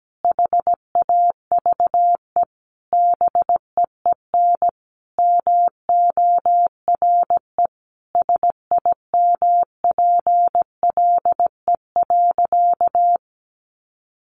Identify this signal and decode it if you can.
Morse code, heard as a keyed tone, about 17 wpm, 700 Hz. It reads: HAVE BEEN MORE SIMPLE.
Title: Sentences from Top 400 Words 17wpm